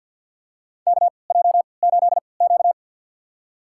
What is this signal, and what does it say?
Morse code, heard as a keyed tone, about 53 wpm, spaced as for 29 wpm, 700 Hz. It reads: KJ8Q
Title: USA Call Signs - 3x Character Spacing - 50wpm